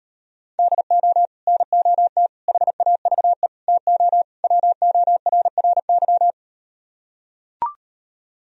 Morse code, heard as a keyed tone, about 39 wpm, 700 Hz. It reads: DO NOT HAVE TO WORRY